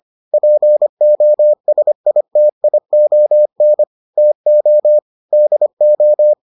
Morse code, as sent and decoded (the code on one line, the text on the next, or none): .--. --- ... .. - .. --- -. / - --- / -.. ---
POSITION TO DO